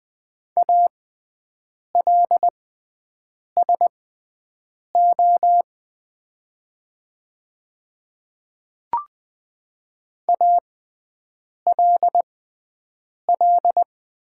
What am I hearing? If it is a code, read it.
Morse code, heard as a keyed tone, about 20 wpm, spaced as for 7 wpm, 700 Hz. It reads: ALSO ALL
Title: 6x Character Spacing - Top 500 Words - 20wpm